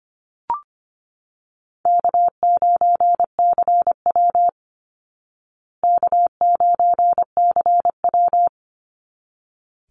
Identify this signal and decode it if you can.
Morse code, heard as a keyed tone, about 25 wpm, 700 Hz. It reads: K9CW K9CW
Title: Easy Call Signs Repeated 2x - 25wpm